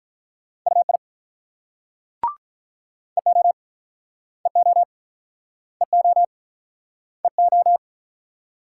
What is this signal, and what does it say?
Morse code, heard as a keyed tone, about 48 wpm, 700 Hz. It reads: UI EO EO EO EO